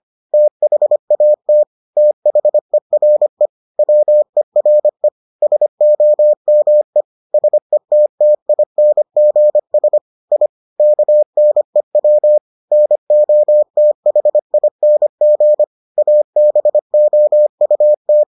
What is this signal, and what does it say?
Morse code, heard as a keyed tone, about 25 wpm, 600 Hz. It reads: THAT THERE WERE SOME SETTINGS I KNEW NOTHING ABOUT